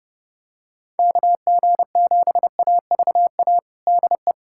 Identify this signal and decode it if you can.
Morse code, heard as a keyed tone, about 30 wpm, 700 Hz. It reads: KG7AVA DE
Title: USA Callsign DE Callsign - 6 Characters - 30wpm